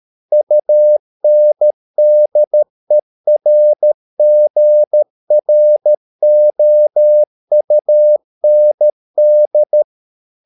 Morse code, heard as a keyed tone, about 13 wpm, 600 Hz. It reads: UNDERGROUND